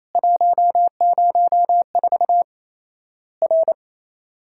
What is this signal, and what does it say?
Morse code, heard as a keyed tone, about 28 wpm, 700 Hz. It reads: 104 R
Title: Worked All Europe DX Contest - QTC Exchange - 28wpm